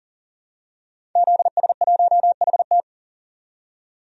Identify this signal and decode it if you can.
Morse code, heard as a keyed tone, about 40 wpm, 700 Hz. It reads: ZS1HT
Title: DXCC-100 Call Signs - 5 Characters - 40wpm